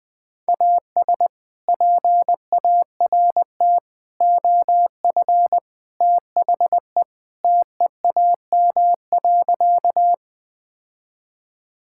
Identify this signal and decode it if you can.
Morse code, heard as a keyed tone, about 19 wpm, 700 Hz. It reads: AS PART OF THE TEAM.
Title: Sentences with 3-Word Phrases - Spoken Beginning - 20wpm